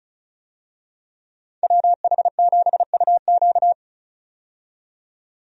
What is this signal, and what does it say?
Morse code, heard as a keyed tone, about 34 wpm, 700 Hz. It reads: WH7UQ
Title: USA Call Signs - 5 Characters - 35wpm